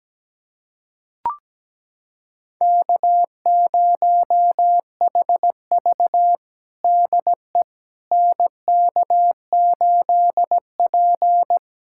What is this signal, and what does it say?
Morse code, heard as a keyed tone, about 17 wpm, 700 Hz. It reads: K0HV DE NK8P
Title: USA Callsign DE Callsign - 4 Characters - 17wpm